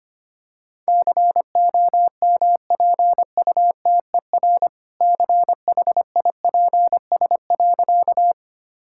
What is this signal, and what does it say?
Morse code, heard as a keyed tone, about 25 wpm, 700 Hz. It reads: COMPUTER CHIPS.
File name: practice / 25wpm / All-Age Appropriate Jokes - 25wpm